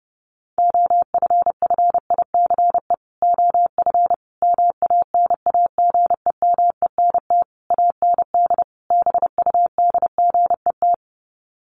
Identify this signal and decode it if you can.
Morse code, heard as a keyed tone, about 30 wpm, 700 Hz. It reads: OFFICE OF MANAGEMENT AND BUDGET
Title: Sets of 5 Words Encore2 30wpm